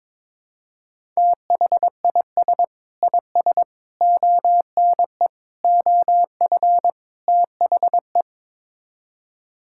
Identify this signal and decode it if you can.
Morse code, heard as a keyed tone, about 22 wpm, 700 Hz. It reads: THIS IS ONE OF THE
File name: practice / 22wpm / Build A Phrase - 5-Word Phrases - 22wpm